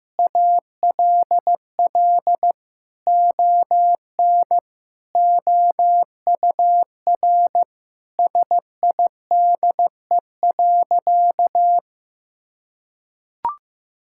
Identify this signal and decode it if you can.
Morse code, heard as a keyed tone, about 15 wpm, 700 Hz. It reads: ALL ON OUR SIDE.
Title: Sentences from Top 1000 Words - Review Part 6 - 15wpm